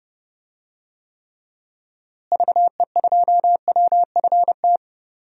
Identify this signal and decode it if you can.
Morse code, heard as a keyed tone, about 32 wpm, 700 Hz. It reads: VE2WFT
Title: DXCC-100 Call Signs - 6 Characters - Repeated 3x - 30wpm